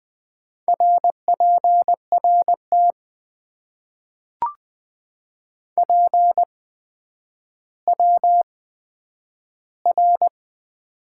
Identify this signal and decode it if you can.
Morse code, heard as a keyed tone, about 20 wpm, 700 Hz. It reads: R P R T P W R
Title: Top 90 QSO Elements - 8x Character Spacing - 20wpm